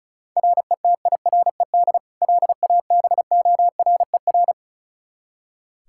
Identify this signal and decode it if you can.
Morse code, heard as a keyed tone, about 36 wpm, 700 Hz. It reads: RETIRED LABORER
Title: Practice QSO - Exchange 3 - More Info - 35wpm